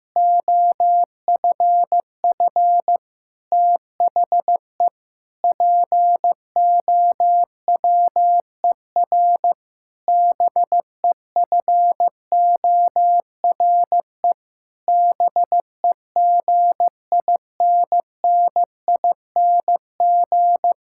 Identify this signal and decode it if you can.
Morse code, heard as a keyed tone, about 15 wpm, 700 Hz. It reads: OFF THE POWER BEFORE BEGINNING
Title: Sentences from Top 1000 Words - Review Part 6 - 15wpm